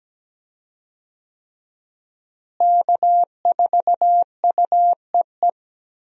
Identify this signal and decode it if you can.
Morse code, heard as a keyed tone, about 17 wpm, 700 Hz. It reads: K4UEE
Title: USA Call Signs - 5 Characters - Repeated 3x - 17wpm